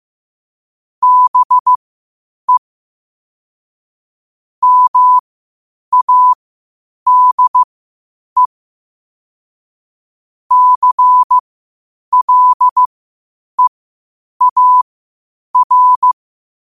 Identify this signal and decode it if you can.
Morse code, heard as a keyed tone, about 15 wpm, spaced as for 8 wpm, 1000 Hz. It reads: BE MADE CLEAR
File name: qotdfiles / 8wpm